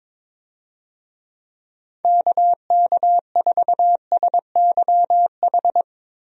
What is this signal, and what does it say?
Morse code, heard as a keyed tone, about 22 wpm, 700 Hz. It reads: KK4SYH
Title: USA Call Signs - 6 Characters - Repeated 3x - 22wpm